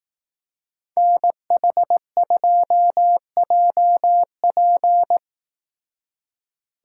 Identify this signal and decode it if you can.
Morse code, heard as a keyed tone, about 18 wpm, 700 Hz. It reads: NH2JP
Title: Speed Racing - USA 2x2 Call Signs 15wpm